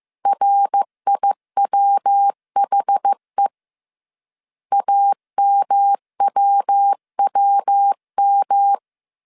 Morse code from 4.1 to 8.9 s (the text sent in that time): AMWWM